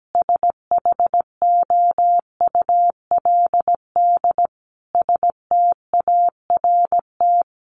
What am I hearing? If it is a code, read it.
Morse code, heard as a keyed tone, about 17 wpm, 700 Hz. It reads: SHOULD START
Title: Sentences from Top 1100 Words 17wpm